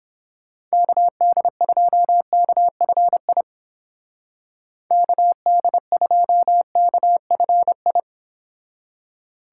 Morse code, heard as a keyed tone, about 29 wpm, 700 Hz. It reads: KD2KFI KD2KFI